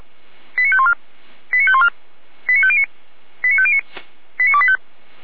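Five-tone selective call (ZVEI), heard as quick runs of five tones, tones of about 70 ms. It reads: 89514 89514 89509 89509 89285
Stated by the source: ZVEI five-tone selective call received on SDR.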